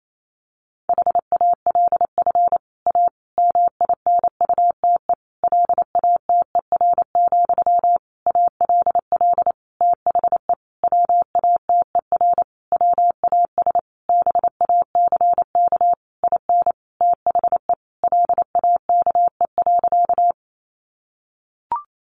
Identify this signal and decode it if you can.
Morse code, heard as a keyed tone, about 28 wpm, 700 Hz. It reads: HALF A MINUTE LATER, ALL THE WATER WAS BACK IN THE LAKE.
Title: Sentences from Top 1100 Words 28wpm